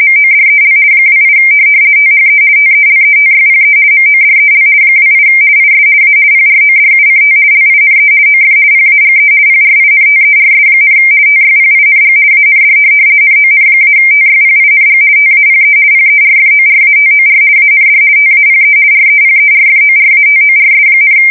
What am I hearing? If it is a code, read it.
RTTY, about 50 baud. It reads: MODEL 28 THE TELETYPE MODEL 28 IS A PRODUCT LINE OF PAGE PRINTERS, TYPING AND NON-TYPING TAPE PERFORATOR AND TAPE REPERFORATORS,
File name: fsk170-50B / teletype-corp